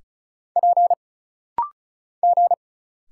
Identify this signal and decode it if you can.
Morse code, heard as a keyed tone, about 36 wpm, 700 Hz. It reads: P G
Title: Single Letters - Rapid-Fire 35wpm